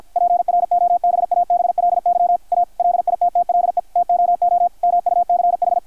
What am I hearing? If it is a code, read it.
Morse code, heard as a keyed tone, about 53 wpm, 720 Hz, with radio noise. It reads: PROBABLY A LITTLE TOO MUCH
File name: archived / CRSnet